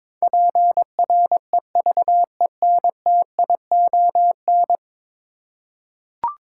Morse code, heard as a keyed tone, about 22 wpm, 700 Hz. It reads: PREVENTION
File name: practice / 22wpm / 10 Letter Words - First 3 Letters Given - 22wpm